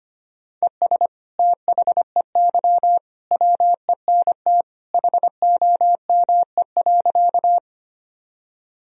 Morse code, heard as a keyed tone, about 25 wpm, 700 Hz. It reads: ES THEY WENT HOME.
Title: Sentences from Top 200 Words and Abbreviations 25wpm